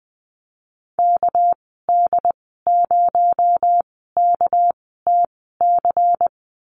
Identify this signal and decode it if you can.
Morse code, heard as a keyed tone, about 20 wpm, spaced as for 15 wpm, 700 Hz. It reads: KD0KTC
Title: 238 Lesson 38g - Letter q - Call Signs 20wpm